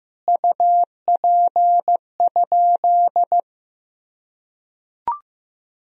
Morse code, heard as a keyed tone, about 15 wpm, 700 Hz. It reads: UP?
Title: Sentences from Top 1000 Words - Review Part 1 - 15wpm